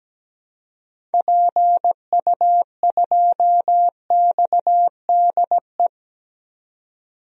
Morse code, heard as a keyed tone, about 17 wpm, 700 Hz. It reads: PU2XDE
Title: DXCC-100 Call Signs - 6 Characters - 17wpm